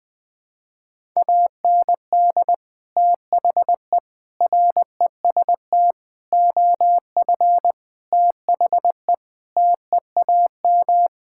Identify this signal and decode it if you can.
Morse code, heard as a keyed tone, about 20 wpm, 700 Hz. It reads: AND THE REST OF THE TEAM
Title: Sentences with 5-Word Phrases - Spoken Beginning - 20wpm